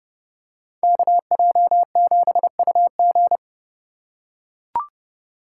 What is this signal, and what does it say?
Morse code, heard as a keyed tone, about 30 wpm, 690 Hz. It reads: KJ7UG
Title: USA Call Signs - 3x Character Spacing - 30wpm